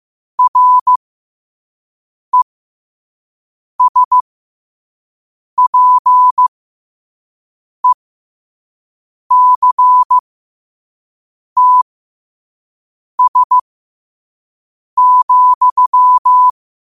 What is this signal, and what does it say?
Morse code, heard as a keyed tone, about 15 wpm, spaced as for 5 wpm, 1000 Hz. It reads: RESPECTS,
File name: qotdfiles / 5wpm